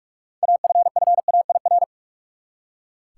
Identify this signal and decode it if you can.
Morse code, heard as a keyed tone, about 46 wpm, 700 Hz. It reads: AFFAIR
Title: Six Letter Words - 45wpm